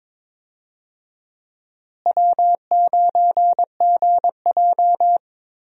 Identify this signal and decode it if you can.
Morse code, heard as a keyed tone, about 22 wpm, 700 Hz. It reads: W9GJ
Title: USA Call Signs - 4 Characters - Repeated 3x - 22wpm